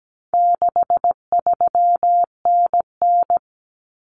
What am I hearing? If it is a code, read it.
Morse code, heard as a keyed tone, about 17 wpm, 700 Hz. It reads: 63NN
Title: Common Cut Numbers - 4-Digit 17wpm